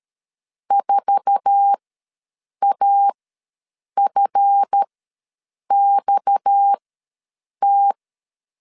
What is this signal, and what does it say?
Morse code, heard as a keyed tone, about 13 wpm, spaced as for 7 wpm, 800 Hz. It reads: 4AFXT
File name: LEZIONI / 8 YJ45 / 8 YJ4 ws 12wpm 3249